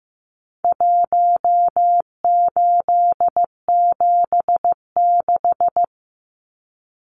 Morse code, heard as a keyed tone, about 15 wpm, 700 Hz. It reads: 1876
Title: Numbers - Contest Style - 4-Digit 15wpm